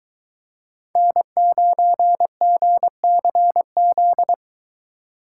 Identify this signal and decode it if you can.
Morse code, heard as a keyed tone, about 22 wpm, 700 Hz. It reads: N9GCZ